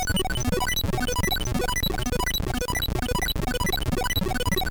There is no code.